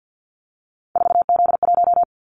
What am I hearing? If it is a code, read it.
Morse code, heard as a keyed tone, about 50 wpm, 700 Hz. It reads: 4Z1
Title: Callsign Prefixes for Top 50 DXCC - 50wpm